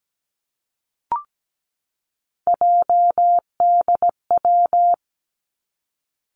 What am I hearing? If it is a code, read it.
Morse code, heard as a keyed tone, about 17 wpm, 700 Hz. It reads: JDW
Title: Three Challenging Letter-Number Combinations 17wpm